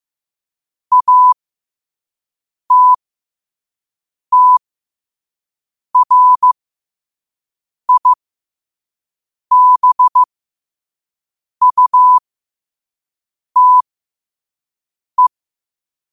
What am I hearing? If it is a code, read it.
Morse code, heard as a keyed tone, about 15 wpm, spaced as for 5 wpm, 1000 Hz. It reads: ATTRIBUTE